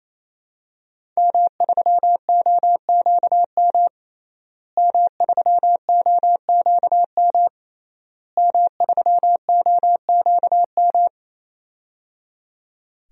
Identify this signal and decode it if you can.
Morse code, heard as a keyed tone, about 28 wpm, 700 Hz. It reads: M3OQM M3OQM M3OQM
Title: DXCC-100 Call Signs - 5 Characters - Repeated 3x - 28wpm